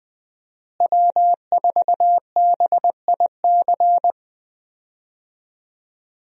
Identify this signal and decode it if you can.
Morse code, heard as a keyed tone, about 21 wpm, 700 Hz. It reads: W4BIC
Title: USA Call Signs - 5 Characters - Repeated 3x - 20wpm